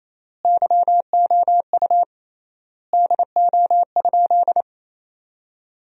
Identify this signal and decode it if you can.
Morse code, heard as a keyed tone, about 28 wpm, 700 Hz. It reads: YOU DO?
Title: 3x Word-Spacing - Sentences from Top 100 Words - 28wpm